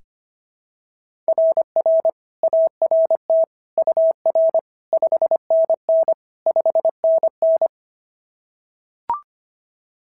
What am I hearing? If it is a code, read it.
Morse code, heard as a keyed tone, about 25 wpm, 690 Hz. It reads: RR ART UR 5NN 5NN
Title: SOTA QSO Practice 25wpm